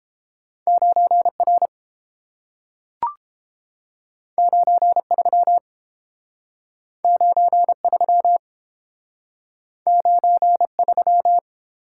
Morse code, heard as a keyed tone, about 30 wpm, 700 Hz. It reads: 9 R 9 3 9 3 9 3